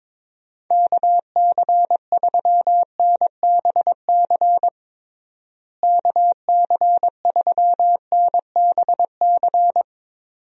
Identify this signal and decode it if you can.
Morse code, heard as a keyed tone, about 22 wpm, 700 Hz. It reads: KC3NBC KC3NBC
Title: USA Call Signs - 6 Characters - Repeated 2x - 22wpm